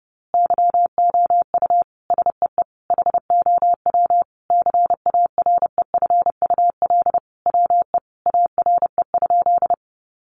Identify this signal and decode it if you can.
Morse code, heard as a keyed tone, about 30 wpm, 700 Hz. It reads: YOU SEE HOW CAREFUL WE ARE?
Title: A Sound Of Thunder 30wpm